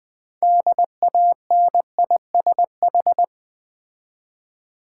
Morse code, heard as a keyed tone, about 20 wpm, 700 Hz. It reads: DANISH
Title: Difficult Words with Hints - 20wpm